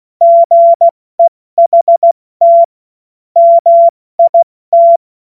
Morse code, heard as a keyed tone, about 16 wpm, 680 Hz. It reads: GEHT MIT
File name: Morsekurs / Witze / Witz 13. April 2021, 70 F80